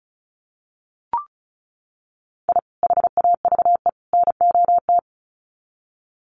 Morse code, heard as a keyed tone, about 34 wpm, 700 Hz. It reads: I HAVE NOT
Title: Sets of 3 Words 35wpm